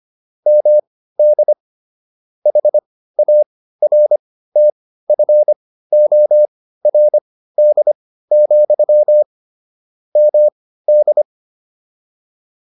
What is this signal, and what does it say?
Morse code, heard as a keyed tone, about 25 wpm, spaced as for 15 wpm, 600 Hz. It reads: MD HARTFORD, MD
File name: QSO401 15